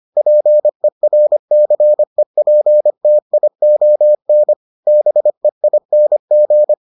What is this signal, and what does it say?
Morse code, heard as a keyed tone, about 25 wpm, 600 Hz. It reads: PERCEPTION BEING